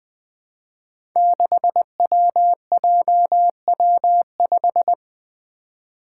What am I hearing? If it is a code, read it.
Morse code, heard as a keyed tone, about 21 wpm, 700 Hz. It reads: 6WJW5
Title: Five Challenging Letter-Number Combinations 20wpm